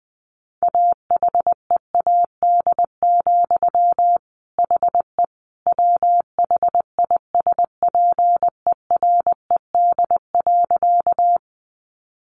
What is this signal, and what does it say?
Morse code, heard as a keyed tone, about 20 wpm, 700 Hz. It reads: AHEAD, HE WHISPERED.
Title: A Sound Of Thunder 20wpm